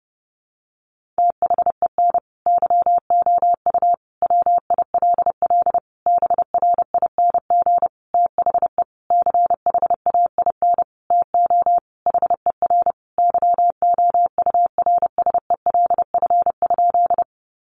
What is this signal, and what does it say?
Morse code, heard as a keyed tone, about 30 wpm, 700 Hz. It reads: THEN YOU WILL BRING THE CHAIN TO HER YOURSELF?
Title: Sentences from Top 1500 Words 30wpm